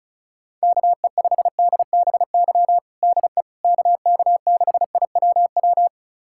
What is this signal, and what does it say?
Morse code, heard as a keyed tone, about 34 wpm, 700 Hz. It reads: KE5DBY DE KK6IWW
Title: USA Callsign DE Callsign - 6 Characters - 35wpm